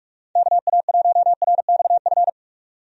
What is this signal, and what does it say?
Morse code, heard as a keyed tone, about 46 wpm, 700 Hz. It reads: KA1RXF
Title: USA Call Signs - 6 Characters - 45wpm